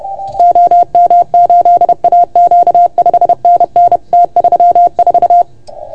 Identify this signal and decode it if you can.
Morse code, heard as a keyed tone, about 30 wpm, 690 Hz. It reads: OM8AQ 5NN T34
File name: Archiv / CUC381 / OK1FGD CUC 381